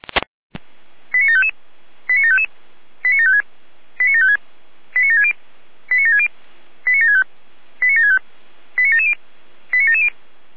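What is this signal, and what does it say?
Five-tone selective call (ZVEI), heard as quick runs of five tones, tones of about 70 ms. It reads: 89755 89755 89756 89756 89760 89760 89765 89765 89770 89770
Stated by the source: ZVEI five-tone selective call received on an SDR.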